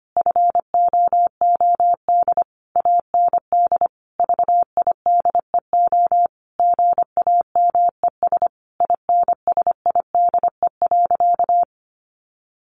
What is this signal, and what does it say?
Morse code, heard as a keyed tone, about 25 wpm, 700 Hz. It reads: FOOD AND VIDEO GAMES INSIDE.
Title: Easy Phrases 25wpm